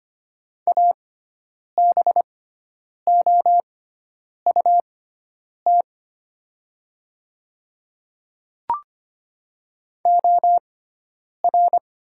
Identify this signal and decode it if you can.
Morse code, heard as a keyed tone, about 25 wpm, spaced as for 9 wpm, 700 Hz. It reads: ABOUT OR
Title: 6x Character Spacing - Top 100 Words - 25wpm